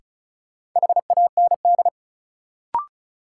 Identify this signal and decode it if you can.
Morse code, heard as a keyed tone, about 36 wpm, 700 Hz. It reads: HAND
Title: Sets of 3 Words - New Word Drill - 35wpm